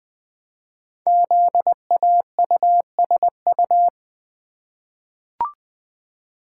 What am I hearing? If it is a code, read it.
Morse code, heard as a keyed tone, about 20 wpm, 700 Hz. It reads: ZAUSU
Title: Code Groups - 5 Letters - 20wpm